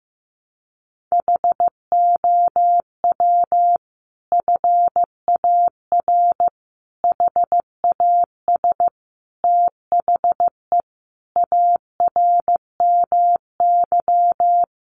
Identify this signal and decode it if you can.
Morse code, heard as a keyed tone, about 15 wpm, 700 Hz. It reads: HOW FAR HAS THE ARMY
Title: Sentences from Top 1200 Words 15wpm